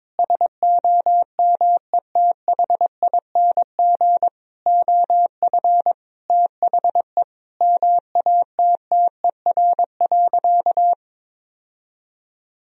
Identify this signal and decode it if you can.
Morse code, heard as a keyed tone, about 22 wpm, 700 Hz. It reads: SOMETHING OF THE MATTER.
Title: Sentences from Top 1000 Words - Review Part 3 - 22wpm